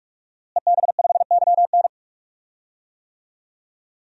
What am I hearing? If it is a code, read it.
Morse code, heard as a keyed tone, about 45 wpm, 700 Hz. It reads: EB5YN
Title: DXCC-100 Call Signs - 5 Characters - Repeated 2x - 45wpm